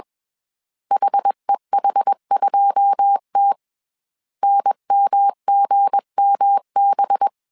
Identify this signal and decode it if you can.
Morse code, heard as a keyed tone, about 21 wpm, 800 Hz. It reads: HEH2T NMGMB